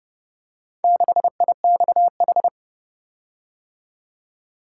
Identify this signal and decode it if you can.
Morse code, heard as a keyed tone, about 29 wpm, 700 Hz. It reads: 6IXH